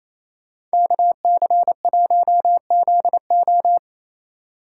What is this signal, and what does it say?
Morse code, heard as a keyed tone, about 33 wpm, 700 Hz. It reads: KC1ZO
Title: Speed Racing - USA 2x2 Call Signs 28wpm